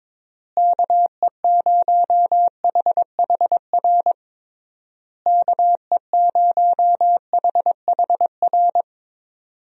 Morse code, heard as a keyed tone, about 22 wpm, 700 Hz. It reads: KE0HHR KE0HHR
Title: USA Call Signs - 6 Characters - Repeated 3x - 22wpm